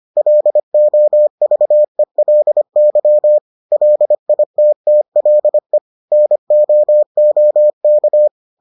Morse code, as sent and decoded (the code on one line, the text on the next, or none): .-.. --- ...- . .-.. -.-- / .-.. .. - - .-.. / -. --- --- -.-
LOVELY LITTLE NOOK